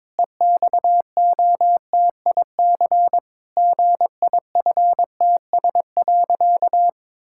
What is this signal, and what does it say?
Morse code, heard as a keyed tone, about 22 wpm, 700 Hz. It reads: EXOTIC GIFTS.
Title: Pangram Challenge v2 - 22wpm